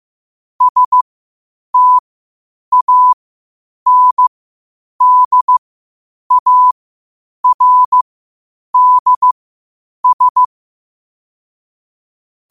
Morse code, heard as a keyed tone, about 15 wpm, spaced as for 8 wpm, 1000 Hz. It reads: STANDARDS
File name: qotdfiles / 8wpm